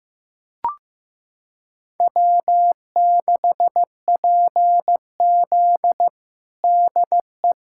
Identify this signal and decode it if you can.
Morse code, heard as a keyed tone, about 15 wpm, 700 Hz. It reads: W6PZ DE
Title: USA Callsign DE Callsign 15wpm